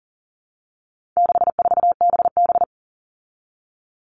Morse code, heard as a keyed tone, about 40 wpm, 700 Hz. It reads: 64BB
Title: All Cut Numbers - 4-Digit 40wpm